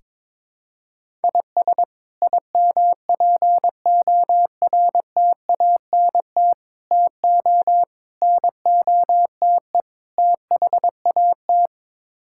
Morse code, heard as a keyed tone, about 22 wpm, 700 Hz. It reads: IS IMPORTANT TO NOTE THAT
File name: practice / 22wpm / Build A Phrase - 5-Word Phrases - 22wpm